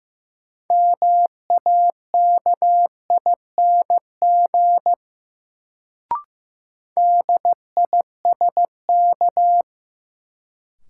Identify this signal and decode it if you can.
Morse code, heard as a keyed tone, about 15 wpm, 700 Hz. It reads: MAKING DISK
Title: Sentences from Top 1000 Words - Review Part 6 - New Word Drill - 15wpm